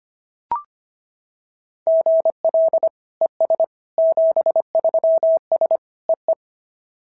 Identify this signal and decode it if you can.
Morse code, heard as a keyed tone, about 25 wpm, 690 Hz. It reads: GL ES 73S EE
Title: SOTA QSO Practice 25wpm